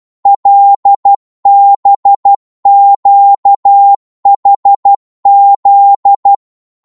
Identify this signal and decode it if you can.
Morse code, heard as a keyed tone, about 12 wpm, 800 Hz. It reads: LBQHZ
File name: Morse / ltr-12wpm